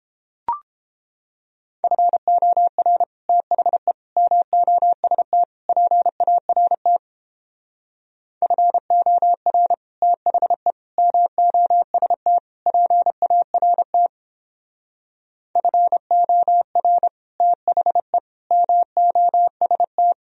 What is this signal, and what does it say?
Morse code, heard as a keyed tone, about 29 wpm, 700 Hz. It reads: FOR THE MOST PART FOR THE MOST PART FOR THE MOST